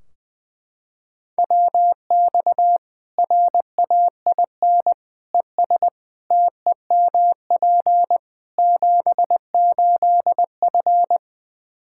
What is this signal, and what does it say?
Morse code, heard as a keyed tone, about 20 wpm, 700 Hz. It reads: WX RAIN ES TEMP 78F
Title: K1USN SST QSO Practice 20wpm